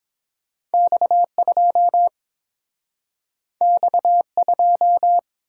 Morse code, heard as a keyed tone, about 30 wpm, 700 Hz. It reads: X2 X2